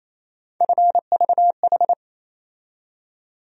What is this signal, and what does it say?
Morse code, heard as a keyed tone, about 28 wpm, 700 Hz. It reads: FVH
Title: USA Call Sign Postfixes - 3 Letters 28wpm